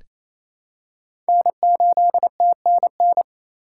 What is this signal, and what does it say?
Morse code, heard as a keyed tone, about 28 wpm, 690 Hz. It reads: N8TNN
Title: USA Call Signs - 5 Characters - 4x Character Spacing - 28wpm